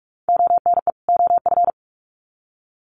Morse code, heard as a keyed tone, about 44 wpm, 700 Hz. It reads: ONE OF
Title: Sets of 2 Words 45wpm